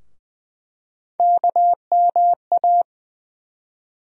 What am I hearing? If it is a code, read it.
Morse code, heard as a keyed tone, about 20 wpm, 700 Hz. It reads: KMA
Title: USA Call Sign Postfixes - 3 Letters 20wpm